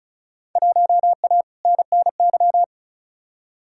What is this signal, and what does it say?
Morse code, heard as a keyed tone, about 36 wpm, 700 Hz. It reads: 1A NNY
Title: Field Day Exchanges - 35wpm